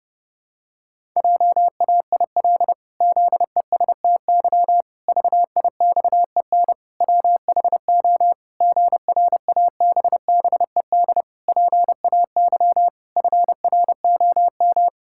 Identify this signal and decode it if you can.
Morse code, heard as a keyed tone, about 30 wpm, 700 Hz. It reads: JAIL ZESTY VIXEN WHO GRABBED PAY FROM
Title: Pangram Challenge v2 - 30wpm